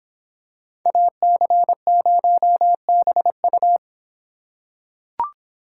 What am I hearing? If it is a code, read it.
Morse code, heard as a keyed tone, about 24 wpm, 700 Hz. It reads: AC0BU